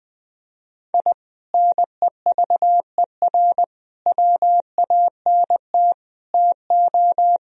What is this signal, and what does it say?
Morse code, heard as a keyed tone, about 20 wpm, 700 Hz. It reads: I NEVER WANT TO
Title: Sentences from Top 1000 Words - Review Part 2 - 20wpm